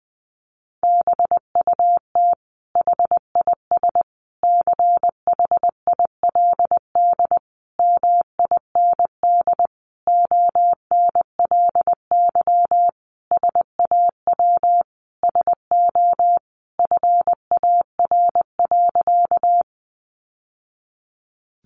Morse code, heard as a keyed tone, about 20 wpm, 700 Hz. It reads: BUT HIS CHILD MIND ONLY SAW SO FAR.
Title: Sentences from Top 800 Words 20wpm